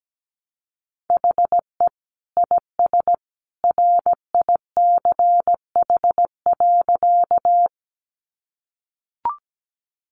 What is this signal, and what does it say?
Morse code, heard as a keyed tone, about 17 wpm, 700 Hz. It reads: HE IS RICH.
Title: Sentences from Top 1100 Words 17wpm